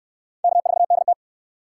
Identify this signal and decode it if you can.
Morse code, heard as a keyed tone, about 59 wpm, 700 Hz. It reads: K4DT